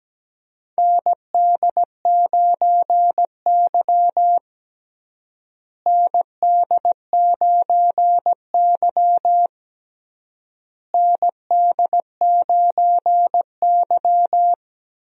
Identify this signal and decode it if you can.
Morse code, heard as a keyed tone, about 17 wpm, 700 Hz. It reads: ND9Y ND9Y ND9Y
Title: USA Call Signs - 4 Characters - Repeated 3x - 17wpm